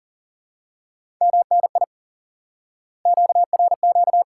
Morse code, heard as a keyed tone, about 39 wpm, 700 Hz. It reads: MNI QRQ
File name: IQE Course 3 - Lesson 89 - YAGI - Two Elements - 40wpm